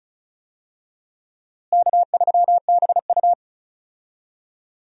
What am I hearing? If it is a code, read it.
Morse code, heard as a keyed tone, about 36 wpm, 700 Hz. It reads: K3BU
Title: USA Call Signs - 4 Characters - Repeated 3x - 35wpm